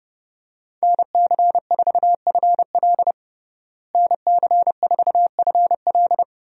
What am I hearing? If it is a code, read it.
Morse code, heard as a keyed tone, about 30 wpm, 700 Hz. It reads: NC4FL NC4FL
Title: USA Call Signs - 5 Characters - Repeated 2x - 30wpm